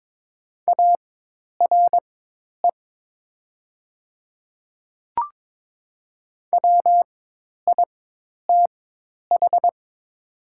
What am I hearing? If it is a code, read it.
Morse code, heard as a keyed tone, about 23 wpm, spaced as for 10 wpm, 700 Hz. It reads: ARE WITH
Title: 4x Character Spacing - Top 100 Words - 22wpm